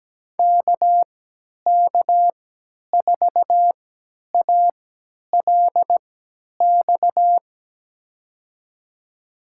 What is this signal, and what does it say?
Morse code, heard as a keyed tone, about 17 wpm, spaced as for 10 wpm, 700 Hz. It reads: KK4ALX
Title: USA Call Signs - 6 Characters - 3x Character Spacing - 17wpm